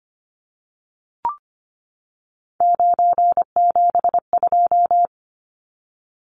Morse code, heard as a keyed tone, about 25 wpm, 690 Hz. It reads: 972
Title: Numbers - 3-Digit 25wpm